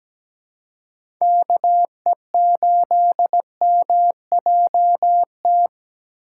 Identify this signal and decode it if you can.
Morse code, heard as a keyed tone, about 17 wpm, 700 Hz. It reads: KE8MJT
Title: USA Call Signs - 6 Characters - 2x Character Spacing - 17wpm